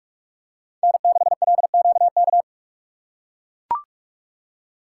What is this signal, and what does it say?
Morse code, heard as a keyed tone, about 42 wpm, 700 Hz. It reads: N6LQK
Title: USA Call Signs - 5 Characters - 3x Character Spacing - 45wpm